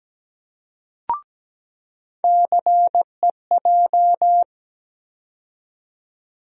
Morse code, heard as a keyed tone, about 17 wpm, 700 Hz. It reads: CEJ
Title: USA Call Sign Postfixes - 3 Letters 17wpm